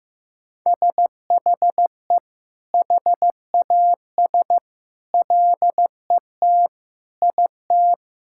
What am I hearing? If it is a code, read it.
Morse code, heard as a keyed tone, about 16 wpm, 700 Hz. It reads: SHE HAS LET IT
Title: Sentences from Top 1000 Words - Review Part 6 - 15wpm